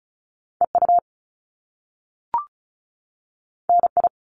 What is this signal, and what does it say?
Morse code, heard as a keyed tone, about 36 wpm, 700 Hz. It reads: EU NI
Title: Top Two Letter Combinations in Words 35wpm